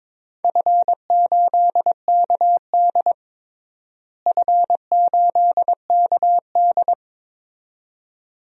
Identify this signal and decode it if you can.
Morse code, heard as a keyed tone, about 22 wpm, 700 Hz. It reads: F8KD F8KD
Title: DXCC-100 Call Signs - 4 Characters - Repeated 2x - 22wpm